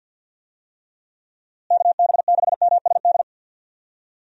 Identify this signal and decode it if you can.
Morse code, heard as a keyed tone, about 52 wpm, 700 Hz. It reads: KB6MSD
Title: USA Call Signs - 6 Characters - Repeated 3x - 50wpm